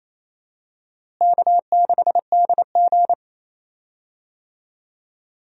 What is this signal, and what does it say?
Morse code, heard as a keyed tone, about 28 wpm, 700 Hz. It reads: K6DG
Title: USA Call Signs - 4 Characters - Repeated 3x - 28wpm